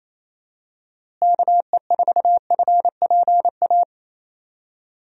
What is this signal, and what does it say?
Morse code, heard as a keyed tone, about 28 wpm, 700 Hz. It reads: KE4FPA
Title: USA Call Signs - 6 Characters - Repeated 3x - 28wpm